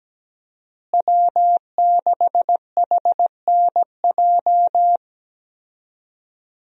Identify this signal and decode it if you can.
Morse code, heard as a keyed tone, about 17 wpm, 700 Hz. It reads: W6HNJ
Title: USA Call Signs - 5 Characters - Repeated 3x - 17wpm